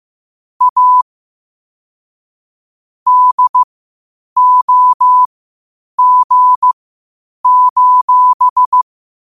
Morse code, heard as a keyed tone, about 15 wpm, spaced as for 8 wpm, 1000 Hz. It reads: A DOG:
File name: qotdfiles / 8wpm